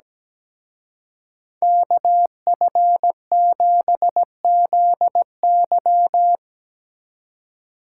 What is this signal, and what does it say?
Morse code, heard as a keyed tone, about 19 wpm, 700 Hz. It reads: KF7ZY
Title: Speed Racing - USA 2x2 Call Signs 17wpm